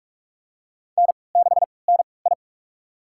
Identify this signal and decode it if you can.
Morse code, heard as a keyed tone, about 47 wpm, spaced as for 26 wpm, 700 Hz. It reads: N6NI
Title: USA Call Signs - 4 Characters - 3x Character Spacing - 45wpm